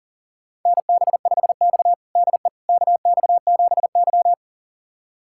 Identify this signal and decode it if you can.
Morse code, heard as a keyed tone, about 40 wpm, 700 Hz. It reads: NB5X DE KX7Y
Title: USA Callsign DE Callsign - 4 Characters - 40wpm